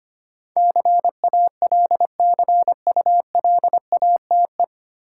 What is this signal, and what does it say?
Morse code, heard as a keyed tone, about 25 wpm, 700 Hz. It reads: CALCULATE
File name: Nine Letter Words - 25wpm